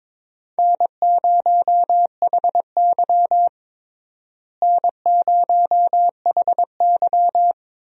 Morse code, heard as a keyed tone, about 21 wpm, 700 Hz. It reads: N0HY N0HY
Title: USA Call Signs - 4 Characters - Repeated 3x - 22wpm